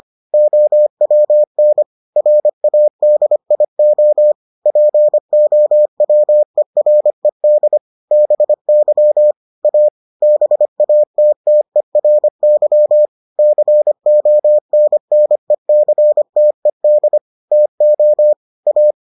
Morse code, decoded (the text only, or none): OWN RADIO POWERED BY A BATTERY CONNECTED TO A